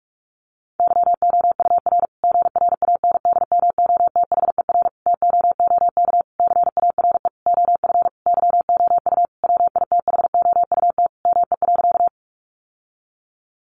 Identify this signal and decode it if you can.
Morse code, heard as a keyed tone, about 44 wpm, 700 Hz. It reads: YOUR GRANDMOTHER TOOK CARE OF YOU WITHOUT ME.
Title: Sentences from Top 1200 Words 45wpm